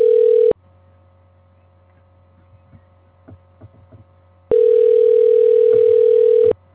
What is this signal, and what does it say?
Ringback tone (North America), 440 + 480 Hz.